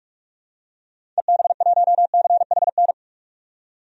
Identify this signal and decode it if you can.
Morse code, heard as a keyed tone, about 46 wpm, 700 Hz. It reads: EB1CHN
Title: DXCC-100 Call Signs - 6 Characters - 45wpm